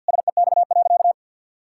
Morse code, heard as a keyed tone, about 50 wpm, 700 Hz. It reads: SEX.
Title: Sentences from Top 1600 Words 50wpm